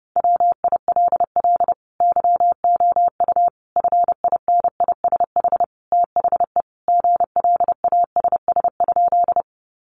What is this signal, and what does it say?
Morse code, heard as a keyed tone, about 30 wpm, 700 Hz. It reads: WILL YOU FINISH THE GLASS?
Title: Sentences from Top 1300 Words 30wpm